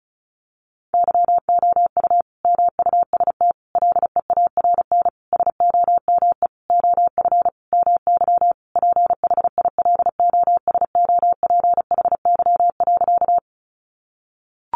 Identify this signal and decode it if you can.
Morse code, heard as a keyed tone, about 34 wpm, 700 Hz. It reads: YOU MUST LEARN SOME OF MY PHILOSOPHY.
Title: Sentences from Top 900 Words 35wpm